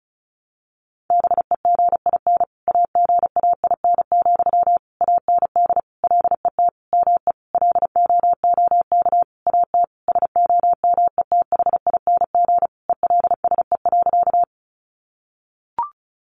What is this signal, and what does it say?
Morse code, heard as a keyed tone, about 34 wpm, 700 Hz. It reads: BEGIN AGAIN, AND LET ME LOOK AT SOMETHING ELSE.
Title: Sentences from Top 800 Words 35wpm